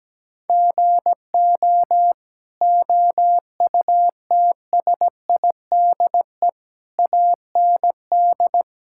Morse code, heard as a keyed tone, about 17 wpm, 700 Hz. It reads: GO OUTSIDE AND
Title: Sentences from Top 1000 Words - Review Part 6 - 17wpm